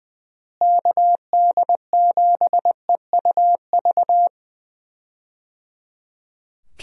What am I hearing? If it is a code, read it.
Morse code, heard as a keyed tone, about 21 wpm, 700 Hz. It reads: KD7EUV
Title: USA Call Signs - 6 Characters - Repeated 3x - 20wpm